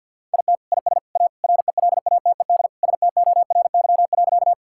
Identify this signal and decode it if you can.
Morse code, heard as a keyed tone, about 50 wpm, 690 Hz. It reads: IT IS A RELATED STORY.
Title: Sentences from Top 1000 Words - Review Part 5 - 50wpm